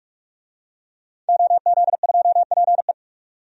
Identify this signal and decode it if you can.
Morse code, heard as a keyed tone, about 43 wpm, 700 Hz. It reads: OZ2PE
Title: DXCC-100 Call Signs - 5 Characters - 45wpm